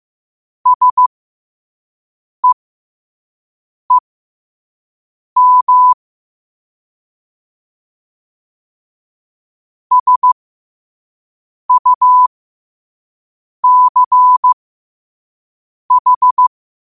Morse code, heard as a keyed tone, about 15 wpm, spaced as for 5 wpm, 1000 Hz. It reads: SEEM SUCH
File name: qotdfiles / 5wpm